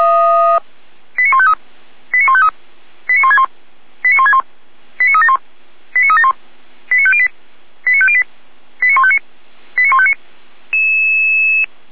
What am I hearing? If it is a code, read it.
Five-tone selective call (ZVEI), heard as quick runs of five tones, tones of about 70 ms. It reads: 89153 89153 89161 89161 89371 89371 89508 89508 89159 89159
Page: SDR/ZVEI